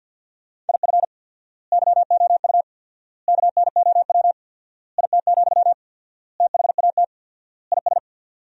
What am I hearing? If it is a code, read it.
Morse code, heard as a keyed tone, about 50 wpm, 700 Hz. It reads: IF YOU KNOW IT, THAT IS
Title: 4x Word-Spacing - Sentences from Top 400 Words - 50wpm